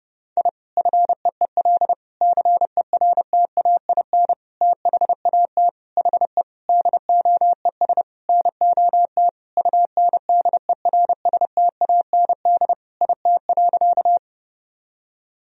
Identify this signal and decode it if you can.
Morse code, heard as a keyed tone, about 29 wpm, 700 Hz. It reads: I FEEL CERTAIN THAT HE DOES NOT UNDERSTAND IT.
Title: Sentences from Top 1000 Words - Review Part 2 - 30wpm